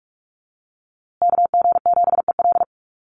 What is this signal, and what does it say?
Morse code, heard as a keyed tone, about 45 wpm, 700 Hz. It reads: KG7EL
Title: USA Call Signs Repeated 3x 45wpm